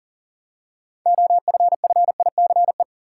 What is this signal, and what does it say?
Morse code, heard as a keyed tone, about 37 wpm, 700 Hz. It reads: OFFICE
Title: Six Letter Words - 40wpm